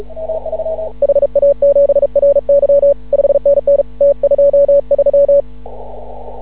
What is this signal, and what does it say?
Morse code, heard as a keyed tone, about 36 wpm, 560 Hz. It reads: HA7RY 5NN T23
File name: CW / ha6oi 2